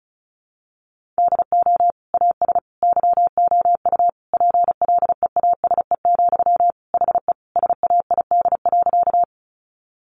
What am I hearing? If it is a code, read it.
Morse code, heard as a keyed tone, about 34 wpm, 700 Hz. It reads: DO AS YOU PLEASE, HE SAID.
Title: Sentences from Top 1500 Words 35wpm